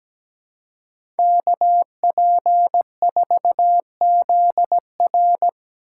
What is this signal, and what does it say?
Morse code, heard as a keyed tone, about 17 wpm, 700 Hz. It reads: KP4ZR
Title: USA Call Signs - 5 Characters - 6x Character Spacing - 17wpm